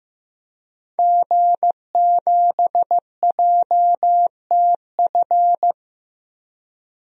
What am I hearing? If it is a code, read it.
Morse code, heard as a keyed tone, about 16 wpm, 700 Hz. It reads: G7JTF
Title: DXCC-100 Call Signs - 5 Characters - Repeated 3x - 15wpm